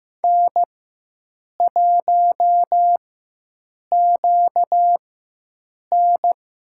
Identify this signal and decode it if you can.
Morse code, heard as a keyed tone, about 15 wpm, spaced as for 7 wpm, 700 Hz. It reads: N1QN
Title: USA Call Signs - 4x Character Spacing - 15wpm